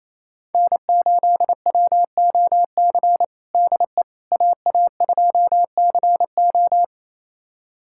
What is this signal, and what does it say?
Morse code, heard as a keyed tone, about 28 wpm, 700 Hz. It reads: N8WOC DE AA2CO
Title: USA Callsign DE Callsign - 5 Characters - 28wpm